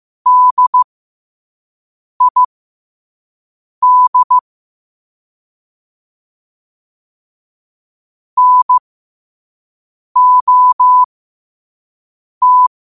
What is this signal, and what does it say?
Morse code, heard as a keyed tone, about 15 wpm, spaced as for 5 wpm, 1000 Hz. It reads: DID NOT
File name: qotdfiles / 5wpm